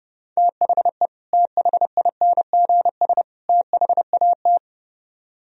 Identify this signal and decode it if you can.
Morse code, heard as a keyed tone, about 35 wpm, 700 Hz. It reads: THE THINGS THAT
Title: Speed-Racing - Sets of 3 Words 30wpm